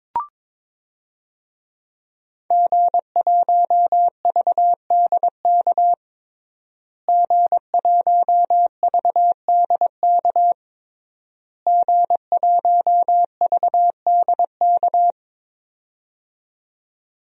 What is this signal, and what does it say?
Morse code, heard as a keyed tone, about 22 wpm, 700 Hz. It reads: G1VDK G1VDK G1VDK
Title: DXCC-100 Call Signs - 5 Characters - Repeated 3x - 22wpm